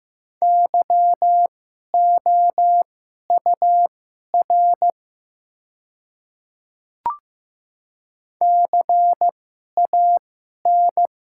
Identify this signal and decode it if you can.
Morse code, heard as a keyed tone, about 15 wpm, spaced as for 11 wpm, 700 Hz. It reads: YOUR CAN
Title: 2x Character Spacing - Top 100 Words - 15wpm